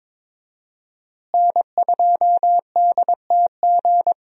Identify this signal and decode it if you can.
Morse code, heard as a keyed tone, about 22 wpm, 700 Hz. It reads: N2DTG
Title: USA Call Signs - 5 Characters - Repeated 3x - 22wpm